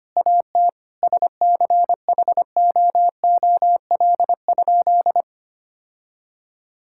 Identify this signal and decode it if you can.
Morse code, heard as a keyed tone, about 25 wpm, 700 Hz. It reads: AT SCHOOL?
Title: Sentences from Top 1000 Words - Review Part 6 - 25wpm